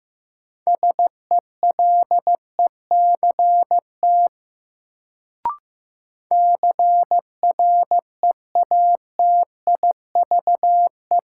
Morse code, heard as a keyed tone, about 15 wpm, 700 Hz. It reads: SELECT CREATIVE
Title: Sentences from Top 1300 Words - New Word Drill Alt - 15wpm